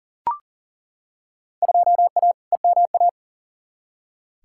Morse code, heard as a keyed tone, about 40 wpm, 700 Hz. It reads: 2A EMA
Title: Field Day Exchanges - 40wpm